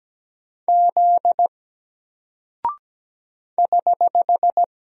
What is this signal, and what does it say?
Morse code, heard as a keyed tone, about 17 wpm, 700 Hz. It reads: Z <HH>
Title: Single Character (Letter:Number:Punctuation:Prosign) 17wpm